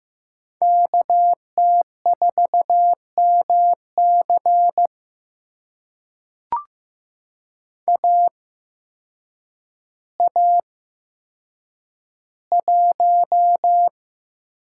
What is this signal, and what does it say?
Morse code, heard as a keyed tone, about 15 wpm, spaced as for 8 wpm, 700 Hz. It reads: KT4MC AA1
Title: USA Call Signs - 5 Characters - 8x Character Spacing - 15wpm